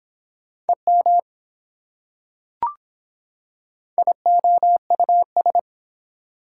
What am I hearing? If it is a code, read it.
Morse code, heard as a keyed tone, about 24 wpm, 700 Hz. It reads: EM IOUS